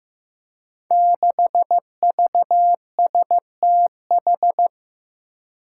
Morse code, heard as a keyed tone, about 15 wpm, 700 Hz. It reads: 6VSTH
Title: Five Challenging Letter-Number Combinations 15wpm